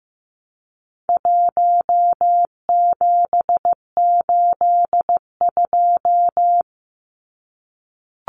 Morse code, heard as a keyed tone, about 15 wpm, 700 Hz. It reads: 1782
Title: Numbers - 4-Digit 15wpm